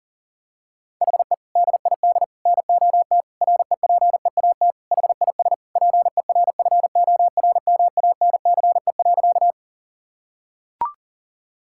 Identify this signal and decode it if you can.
Morse code, heard as a keyed tone, about 40 wpm, 700 Hz. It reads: HE DID NOT REPEAT HIS PERFORMANCE.
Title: Sentences from Top 1600 Words 40wpm